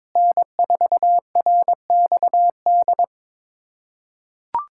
Morse code, heard as a keyed tone, about 22 wpm, 700 Hz. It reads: N4RXD
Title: USA Callsign DE Callsign - 5 Characters - 22wpm